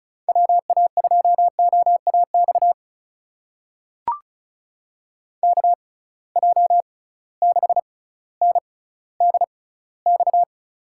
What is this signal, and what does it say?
Morse code, heard as a keyed tone, about 34 wpm, spaced as for 18 wpm, 700 Hz. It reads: WA2OAX KJ6NDX
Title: USA Call Signs - 6 Characters - 6x Character Spacing - 35wpm